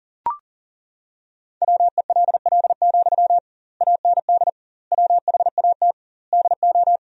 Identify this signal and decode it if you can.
Morse code, heard as a keyed tone, about 40 wpm, 700 Hz. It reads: WELL, AND WHAT DO
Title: 2x Word-Spacing - Sentences from Top 200 Words - 40wpm